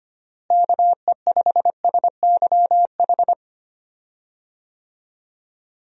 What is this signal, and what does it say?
Morse code, heard as a keyed tone, about 25 wpm, 700 Hz. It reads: KE5SYH
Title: USA Call Signs - 6 Characters - Repeated 2x - 25wpm